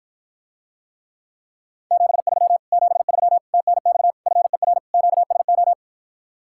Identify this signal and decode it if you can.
Morse code, heard as a keyed tone, about 56 wpm, 700 Hz. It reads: 73 73 TNX FER QSO